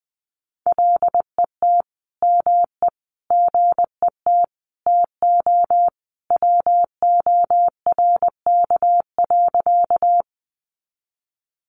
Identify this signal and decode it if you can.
Morse code, heard as a keyed tone, about 20 wpm, 700 Hz. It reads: LET ME GET TO WORK.
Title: Sentences from Top 300 Words 20wpm